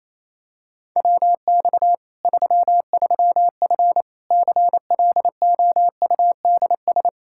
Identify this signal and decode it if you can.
Morse code, heard as a keyed tone, about 28 wpm, 700 Hz. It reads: WX 33F CLOUDS
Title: Practice QSO - Exchange 2 - Info - 28wpm